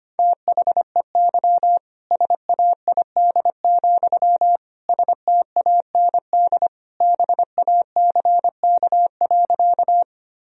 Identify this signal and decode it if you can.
Morse code, heard as a keyed tone, about 25 wpm, 700 Hz. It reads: THEY SAID, STAND BACK.
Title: Sentences from Top 1000 Words - Review Part 1 - 25wpm